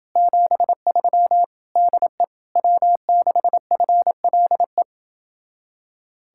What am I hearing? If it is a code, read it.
Morse code, heard as a keyed tone, about 29 wpm, 700 Hz. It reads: 73 DE W6FLE